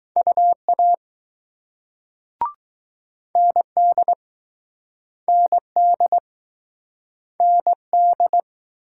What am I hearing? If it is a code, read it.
Morse code, heard as a keyed tone, about 20 wpm, 700 Hz. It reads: UA ND ND ND